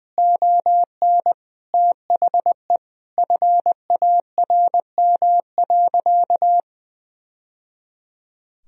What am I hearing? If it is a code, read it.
Morse code, heard as a keyed tone, about 20 wpm, 700 Hz. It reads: ON THE FARM.
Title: Sentences from Top 1000 Words - Review Part 2 - 20wpm